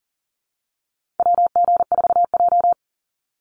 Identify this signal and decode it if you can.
Morse code, heard as a keyed tone, about 40 wpm, 700 Hz. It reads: WG4J
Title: USA Call Signs 40wpm